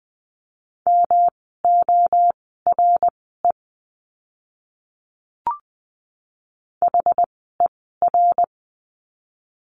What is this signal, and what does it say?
Morse code, heard as a keyed tone, about 20 wpm, spaced as for 15 wpm, 700 Hz. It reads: MORE HER
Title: 108 Lesson 18f - Letter m - Words 20wpm